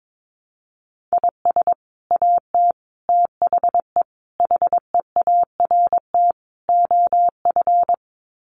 Morse code, heard as a keyed tone, about 22 wpm, 700 Hz. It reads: IS AT THE HEART OF
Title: Sets of 5 Words Encore2 22wpm